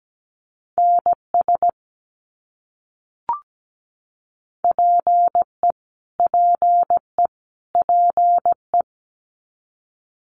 Morse code, heard as a keyed tone, about 17 wpm, 700 Hz. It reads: NS PE PE PE
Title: Canadian Provinces and Territories - Repeated 3x - 17wpm